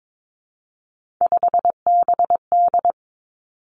Morse code, heard as a keyed tone, about 22 wpm, 700 Hz. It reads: 5BD
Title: Three Challenging Letter-Number Combinations 22wpm